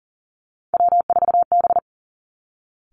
Morse code, heard as a keyed tone, about 41 wpm, 700 Hz. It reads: W4B
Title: Three Challenging Letter-Number Combinations 40wpm